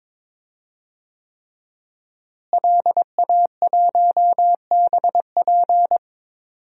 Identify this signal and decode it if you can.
Morse code, heard as a keyed tone, about 22 wpm, 700 Hz. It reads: LA1BP
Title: DXCC-100 Call Signs - 5 Characters - Repeated 3x - 22wpm